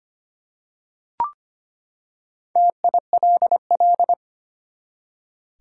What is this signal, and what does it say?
Morse code, heard as a keyed tone, about 25 wpm, 700 Hz. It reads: TILL
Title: Four Letter Words - 25wpm